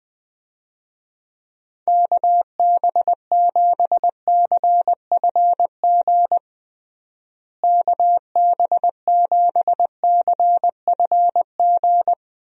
Morse code, heard as a keyed tone, about 20 wpm, 700 Hz. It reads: KB7CFG KB7CFG
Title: USA Call Signs - 6 Characters - Repeated 3x - 20wpm